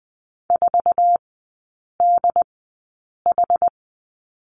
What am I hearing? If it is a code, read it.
Morse code, heard as a keyed tone, about 20 wpm, spaced as for 8 wpm, 700 Hz. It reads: 4DH
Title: 067 Lesson 12e - Letter d - Three Characters 20wpm